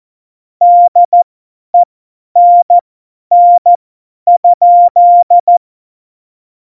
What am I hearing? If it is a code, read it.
Morse code, heard as a keyed tone, about 14 wpm, spaced as for 10 wpm, 700 Hz. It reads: DENN?
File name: Morsekurs / Witze / Witz 18. Juni 2019, 50 F70